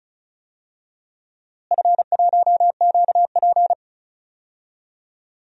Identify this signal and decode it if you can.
Morse code, heard as a keyed tone, about 34 wpm, 700 Hz. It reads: F1QP
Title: DXCC-100 Call Signs - 4 Characters - Repeated 2x - 35wpm